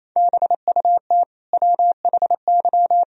Morse code, heard as a keyed tone, about 28 wpm, 700 Hz. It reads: BUT WHY
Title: Sentences from Top 400 Words and Abbreviations 28wpm